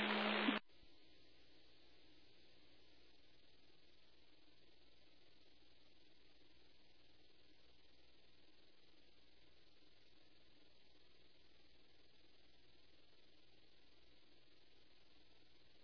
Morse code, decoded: none

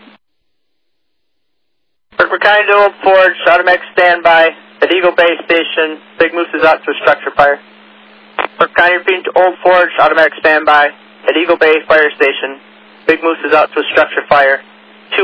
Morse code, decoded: none

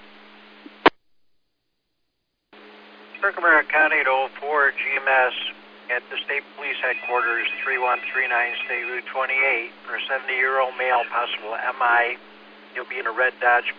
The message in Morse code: none

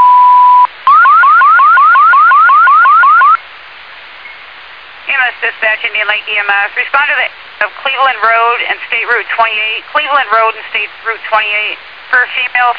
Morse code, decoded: none

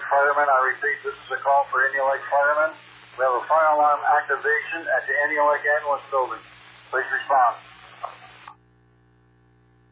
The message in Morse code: none

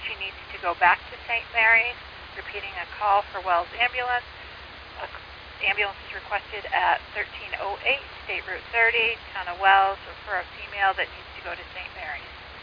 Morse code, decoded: none

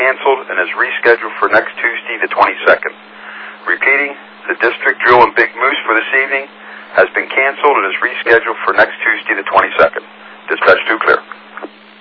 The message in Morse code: none

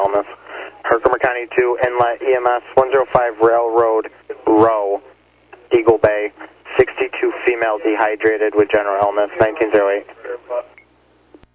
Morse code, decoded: none

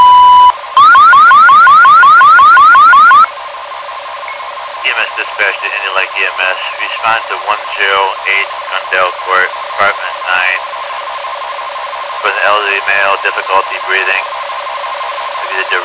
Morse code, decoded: none

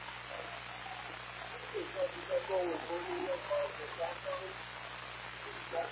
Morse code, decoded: none

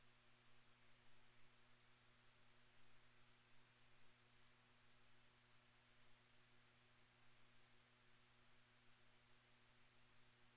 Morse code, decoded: none

